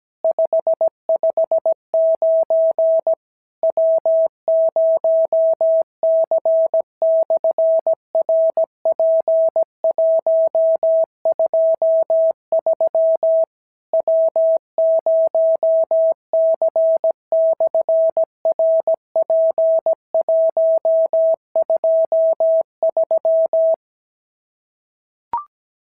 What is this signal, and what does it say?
Morse code, heard as a keyed tone, about 17 wpm, 650 Hz. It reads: 559 W0C/RP123 W0C/RP123
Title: SOTA QSO Practice 17wpm